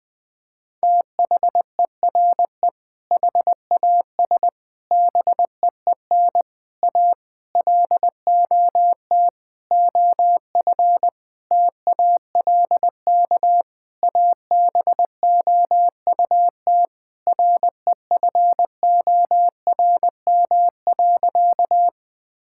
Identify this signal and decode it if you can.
Morse code, heard as a keyed tone, about 20 wpm, 700 Hz. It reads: THERE HAS BEEN A LOT OF TALK ABOUT REFORM.
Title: Sentences with 7-Word Phrases - 20wpm